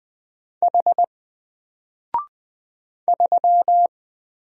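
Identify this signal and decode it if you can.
Morse code, heard as a keyed tone, about 20 wpm, 700 Hz. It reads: H 3
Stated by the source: Single letter and number Morse drill at 20 wpm, from N4LCD.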